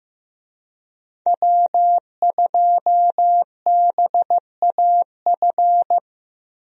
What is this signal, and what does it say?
Morse code, heard as a keyed tone, about 15 wpm, 700 Hz. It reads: W2BAF
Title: USA Call Signs - 5 Characters - 15wpm